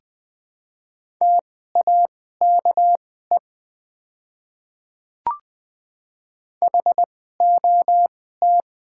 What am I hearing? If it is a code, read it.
Morse code, heard as a keyed tone, about 20 wpm, spaced as for 14 wpm, 700 Hz. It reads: TAKE HOT
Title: 2x Character Spacing - Top 200 Words - 20wpm